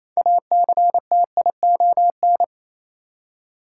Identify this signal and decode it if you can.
Morse code, heard as a keyed tone, about 28 wpm, 700 Hz. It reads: ACTION
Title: Six Letter Words - 28wpm